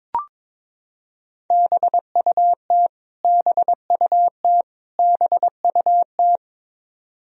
Morse code, heard as a keyed tone, about 22 wpm, 700 Hz. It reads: BUT BUT BUT
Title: Top 100 Words Repeated 22wpm